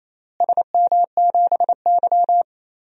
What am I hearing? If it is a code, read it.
Morse code, heard as a keyed tone, about 28 wpm, 700 Hz. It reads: SM7Y
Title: DXCC-100 Call Signs - 4 Characters - Repeated 3x - 28wpm